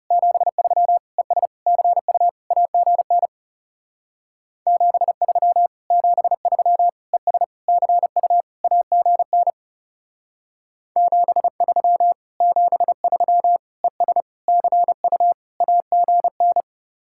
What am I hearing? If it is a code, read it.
Morse code, heard as a keyed tone, about 39 wpm, 700 Hz. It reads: 73 ES CU AGN 73 73 ES CU AGN 73 73 ES CU AGN